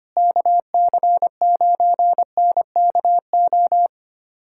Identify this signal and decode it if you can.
Morse code, heard as a keyed tone, about 25 wpm, 700 Hz. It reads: KC9NKO
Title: USA Callsign DE Callsign - 6 Characters - 25wpm